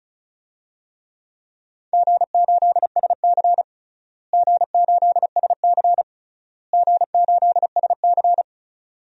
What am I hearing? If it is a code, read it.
Morse code, heard as a keyed tone, about 36 wpm, 700 Hz. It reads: G8SC G8SC G8SC
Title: DXCC-100 Call Signs - 4 Characters - Repeated 3x - 35wpm